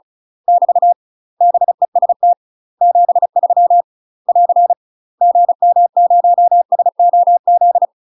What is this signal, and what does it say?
Morse code, heard as a keyed tone, about 35 wpm, 700 Hz. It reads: <BT> BEST 73 <AR> GM0SOZ
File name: UK MASTER QSOs TRAINING / 35WPM / UK MASTER QSOs 0003